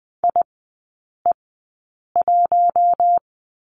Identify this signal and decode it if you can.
Morse code, heard as a keyed tone, about 20 wpm, spaced as for 8 wpm, 700 Hz. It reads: IE1
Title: 067 Lesson 12e - Letter d - Three Characters 20wpm